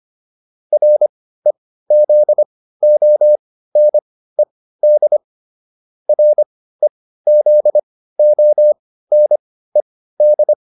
Morse code, heard as a keyed tone, about 25 wpm, spaced as for 15 wpm, 600 Hz. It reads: REZONED REZONED